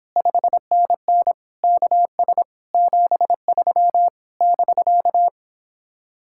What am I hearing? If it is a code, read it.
Morse code, heard as a keyed tone, about 31 wpm, 700 Hz. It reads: 5NN KS 73 <BK>